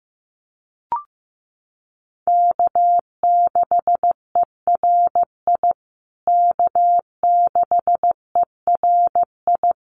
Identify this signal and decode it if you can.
Morse code, heard as a keyed tone, about 15 wpm, 700 Hz. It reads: K6ERI K6ERI
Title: USA Call Signs Repeated 3x 15wpm